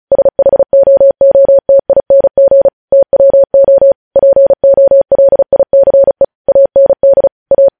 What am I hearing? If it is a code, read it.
Morse code, heard as a keyed tone, about 36 wpm, 550 Hz. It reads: SHOOTING TWO POLICE AND A